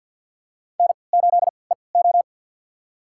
Morse code, heard as a keyed tone, about 53 wpm, spaced as for 29 wpm, 700 Hz. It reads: N8EO
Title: USA Call Signs - 4 Characters - 3x Character Spacing - 50wpm